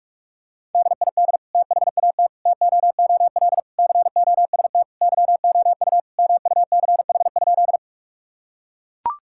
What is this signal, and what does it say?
Morse code, heard as a keyed tone, about 41 wpm, 700 Hz. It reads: DID THAT TOOL COST YOU MUCH?
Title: Sentences from Top 1000 Words - Review Part 6 - 45wpm